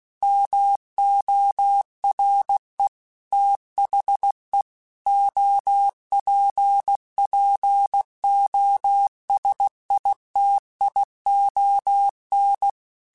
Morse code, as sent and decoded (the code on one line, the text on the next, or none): -- --- .-. / - .... / --- .--. .--. --- ... .. - .. --- -.
MORE THE OPPOSITION